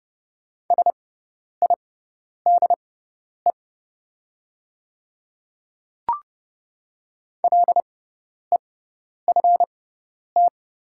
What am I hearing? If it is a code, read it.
Morse code, heard as a keyed tone, about 30 wpm, spaced as for 10 wpm, 690 Hz. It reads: SIDE LEFT
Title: 6x Character Spacing - Top 300 Words - 30wpm